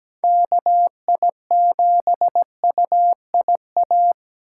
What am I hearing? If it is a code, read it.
Morse code, heard as a keyed tone, about 20 wpm, 700 Hz. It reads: KI7UIA
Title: Speed Racing - USA Call Signs 17wpm